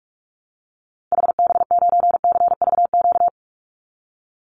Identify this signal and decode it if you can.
Morse code, heard as a keyed tone, about 44 wpm, 700 Hz. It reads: HB9CVQ
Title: Call Signs Repeated 3x - 45wpm